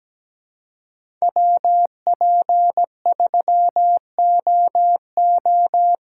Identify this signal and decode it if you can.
Morse code, heard as a keyed tone, about 20 wpm, 700 Hz. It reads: WP3OO
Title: Speed Racing - USA 2x2 Call Signs 17wpm